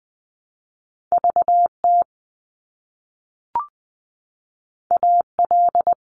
Morse code, heard as a keyed tone, about 20 wpm, 700 Hz. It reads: VT AL
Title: US States Repeated 3x 20wpm